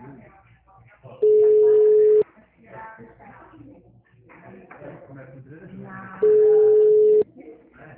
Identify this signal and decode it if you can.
Ringback tone (Europe, 425 Hz), with speech.